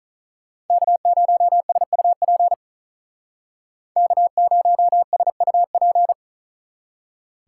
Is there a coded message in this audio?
Yes: Morse code from 0.7 to 6.1 s, heard as a keyed tone, about 45 wpm, 700 Hz. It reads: K0SUP K0SUP